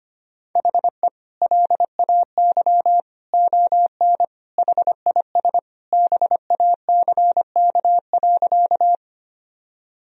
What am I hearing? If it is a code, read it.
Morse code, heard as a keyed tone, about 25 wpm, 700 Hz. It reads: HE LAY ON HIS BACK.
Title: Sentences from Top 1000 Words - Review Part 2 - 25wpm